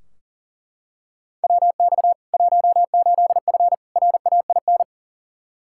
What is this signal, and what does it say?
Morse code, heard as a keyed tone, about 40 wpm, 700 Hz. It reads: WX 18F RAIN
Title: Practice QSO - Exchange 2 - Info - 40wpm